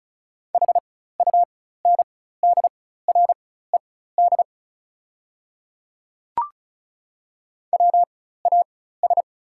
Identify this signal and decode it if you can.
Morse code, heard as a keyed tone, about 34 wpm, spaced as for 16 wpm, 690 Hz. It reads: HUNDRED WAS
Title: 4x Character Spacing - Top 500 Words - 35wpm